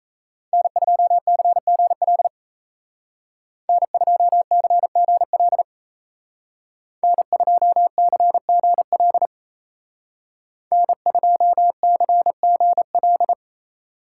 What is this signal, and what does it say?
Morse code, heard as a keyed tone, about 34 wpm, 700 Hz. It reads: N2CGL N2CGL N2CGL N2CGL
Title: Speed Racing - USA 1x3 Call Signs 28wpm